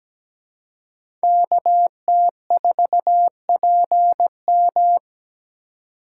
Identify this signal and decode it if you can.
Morse code, heard as a keyed tone, about 17 wpm, 700 Hz. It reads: KT4PM
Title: USA Call Signs - 2x Character Spacing - 17wpm